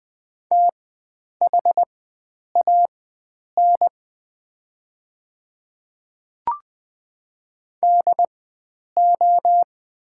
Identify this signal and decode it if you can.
Morse code, heard as a keyed tone, about 20 wpm, spaced as for 9 wpm, 690 Hz. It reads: THAN DO
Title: 4x Character Spacing - Top 100 Words - 20wpm